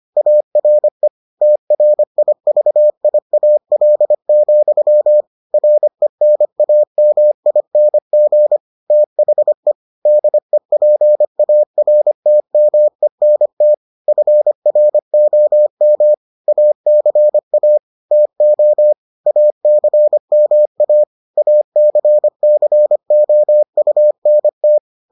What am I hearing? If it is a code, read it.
Morse code, heard as a keyed tone, about 25 wpm, 600 Hz. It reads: ARE TRIVIAL, RENAMING THE DEPARTMENT FROM ACA TO ACMA ACCOUNT